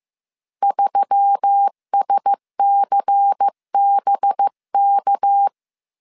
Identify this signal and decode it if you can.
Morse code, heard as a keyed tone, about 15 wpm, 800 Hz. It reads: 3SCBK
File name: CW / DET / 1 6 1500C S C 14wpm 2951